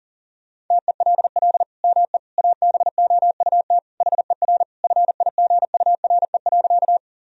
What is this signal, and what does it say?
Morse code, heard as a keyed tone, about 40 wpm, 700 Hz. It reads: TELL ME ABOUT HER FIGURE.
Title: Sentences from Top 1000 Words - Review Part 2 - 40wpm